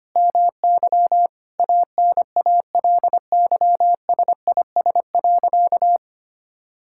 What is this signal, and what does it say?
Morse code, heard as a keyed tone, about 25 wpm, 700 Hz. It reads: MY ANALYSIS.
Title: Sentences from Top 1000 Words - Review Part 5 - 25wpm